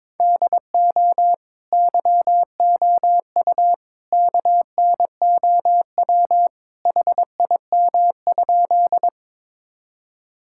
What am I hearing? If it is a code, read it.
Morse code, heard as a keyed tone, about 22 wpm, 700 Hz. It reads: DO YOU KNOW HIM?
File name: practice / 22wpm / Sentences with 2-Word Phrases - 22wpm